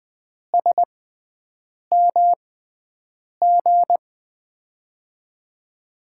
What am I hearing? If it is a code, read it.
Morse code, heard as a keyed tone, about 20 wpm, spaced as for 7 wpm, 700 Hz. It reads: SMG
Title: USA Call Signs - 5 Characters - 6x Character Spacing - 20wpm